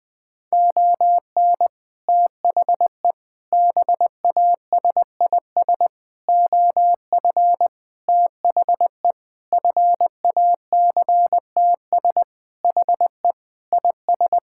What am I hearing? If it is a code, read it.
Morse code, heard as a keyed tone, about 20 wpm, 700 Hz. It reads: ON THE BASIS OF THE FACTS HE IS
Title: Sentences with 3-Word Phrases - Spoken Beginning - 20wpm